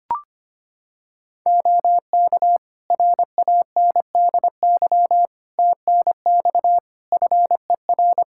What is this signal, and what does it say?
Morse code, heard as a keyed tone, about 25 wpm, 700 Hz. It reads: OK RANDY TNX FER
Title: Practice QSO - Exchange 4 - Ending - 25wpm